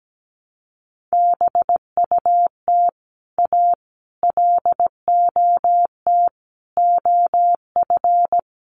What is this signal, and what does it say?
Morse code, heard as a keyed tone, about 17 wpm, 700 Hz. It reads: BUT A LOT OF
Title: Sets of 5 Words Encore2 17wpm